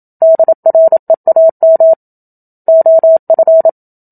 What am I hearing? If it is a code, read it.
Morse code, heard as a keyed tone, about 27 wpm, 650 Hz. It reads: DREAM OF
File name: Book Two / 01-Chapter 1 - 27 WPM